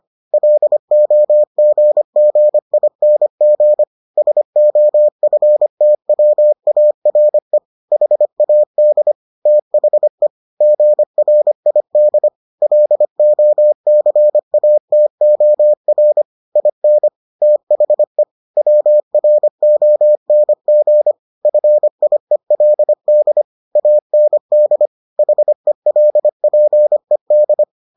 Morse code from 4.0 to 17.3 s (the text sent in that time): SOFTWARE HAD THE GRID LOCATOR IN